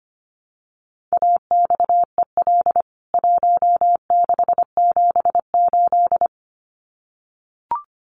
Morse code, heard as a keyed tone, about 25 wpm, 700 Hz. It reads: AXEL 1678
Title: CWT Exchange 25wpm